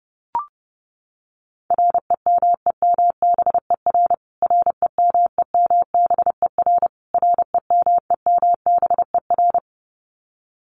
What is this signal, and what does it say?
Morse code, heard as a keyed tone, about 30 wpm, 700 Hz. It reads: REMEMBER REMEMBER REMEMBER
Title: Top 500 Words Repeated 30wpm